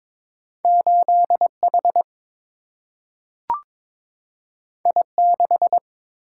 Morse code, heard as a keyed tone, about 22 wpm, 700 Hz. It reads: 8H I6
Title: Single-Letter Single-Number 22wpm